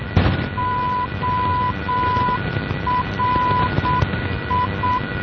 Morse code, heard as a keyed tone, about 7 wpm, 1000 Hz, with noise. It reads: ORI
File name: ORI NDB 1988